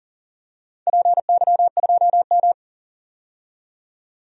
Morse code, heard as a keyed tone, about 40 wpm, 700 Hz. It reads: PY2M
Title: DXCC-100 Call Signs - 4 Characters - 40wpm